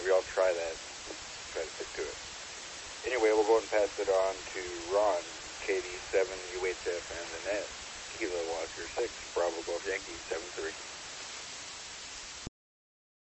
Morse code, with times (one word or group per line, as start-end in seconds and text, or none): none